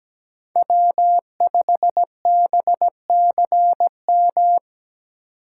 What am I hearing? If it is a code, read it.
Morse code, heard as a keyed tone, about 20 wpm, 700 Hz. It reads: W5BCM
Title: Speed Racing - USA Call Signs 17wpm